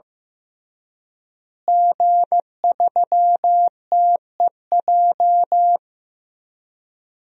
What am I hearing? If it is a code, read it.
Morse code, heard as a keyed tone, about 16 wpm, 700 Hz. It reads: G3TEJ
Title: DXCC-100 Call Signs - 5 Characters - Repeated 3x - 15wpm